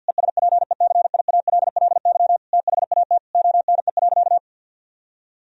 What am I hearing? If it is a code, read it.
Morse code, heard as a keyed tone, about 50 wpm, 700 Hz. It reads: ESPECIALLY THAT ONE.
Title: Sentences from Top 1000 Words - Review Part 4 - 50wpm